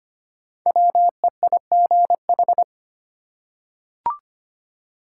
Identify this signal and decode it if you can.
Morse code, heard as a keyed tone, about 25 wpm, 700 Hz. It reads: WEIGH
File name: 5 Letter Words - First 3 Letters Given - 25wpm